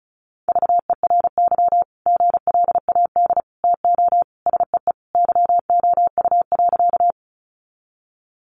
Morse code, heard as a keyed tone, about 34 wpm, 700 Hz. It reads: VERY GLAD TO SEE YOU.
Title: Sentences from Top 800 Words 35wpm